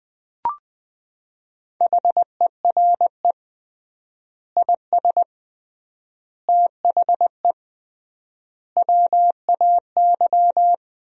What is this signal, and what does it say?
Morse code, heard as a keyed tone, about 21 wpm, 700 Hz. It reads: HERE IS THE WAY
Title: 3x Word-Spacing - Sentences from Top 200 Words - 20wpm